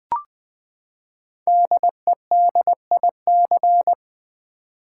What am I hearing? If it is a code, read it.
Morse code, heard as a keyed tone, about 20 wpm, 700 Hz. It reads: DEDIC
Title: Build A Word - 5 Letters At A Time - 10-Letter Words - 20wpm